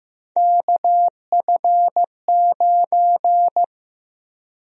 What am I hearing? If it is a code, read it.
Morse code, heard as a keyed tone, about 15 wpm, 700 Hz. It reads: KF9
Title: USA Call Sign Prefixes - Repeated 2x - 15wpm